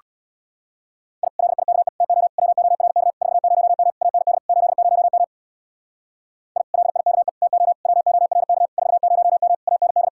Morse code, heard as a keyed tone, about 59 wpm, 700 Hz. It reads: I LIKE TO WORK 40M AND 80M I LIKE TO WORK 40M AND
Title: Speed-Racing Common QSO Phrases - 50wpm